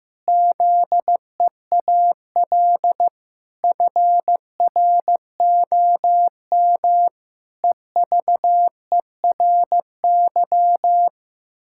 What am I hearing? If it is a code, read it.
Morse code, heard as a keyed tone, about 16 wpm, 700 Hz. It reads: ZEAL FROM EVERY
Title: Pangram Challenge v2 - 15wpm